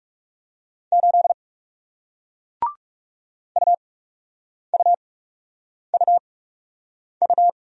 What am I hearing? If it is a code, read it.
Morse code, heard as a keyed tone, about 38 wpm, spaced as for 8 wpm, 700 Hz. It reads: 8 UUUU